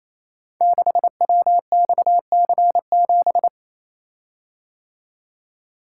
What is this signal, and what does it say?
Morse code, heard as a keyed tone, about 29 wpm, 700 Hz. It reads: 6WXC7
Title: Code Groups - 5 Letter:Numbers - 28wpm